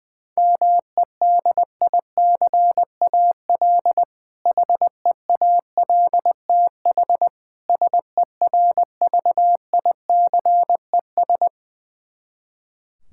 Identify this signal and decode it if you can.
Morse code, heard as a keyed tone, about 21 wpm, 700 Hz. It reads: MEDICAL HEALTH SERVICES
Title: K1USN SST QSO Practice 20wpm